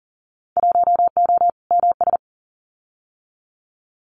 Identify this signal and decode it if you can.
Morse code, heard as a keyed tone, about 40 wpm, 700 Hz. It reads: 1O MS
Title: Winter Field Day Exchanges 40wpm